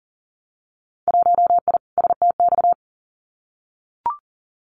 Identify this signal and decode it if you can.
Morse code, heard as a keyed tone, about 40 wpm, 700 Hz. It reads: 1I STX
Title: Winter Field Day Exchanges 40wpm